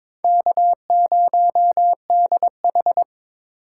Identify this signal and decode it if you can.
Morse code, heard as a keyed tone, about 23 wpm, 700 Hz. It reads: K0DH
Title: USA Call Signs - 3x Character Spacing - 22wpm